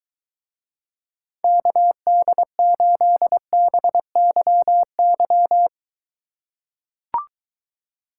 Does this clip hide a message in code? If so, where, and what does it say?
Yes: Morse code from 1.4 to 5.7 s, heard as a keyed tone, about 23 wpm, 700 Hz. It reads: KD8BYY